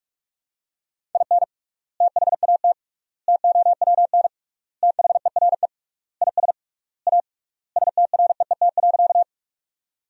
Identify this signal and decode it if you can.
Morse code, heard as a keyed tone, about 46 wpm, 700 Hz. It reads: IN THAT TOWN THERE IS A STREET.
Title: 3x Word-Spacing - Sentences from Top 500 Words - 45wpm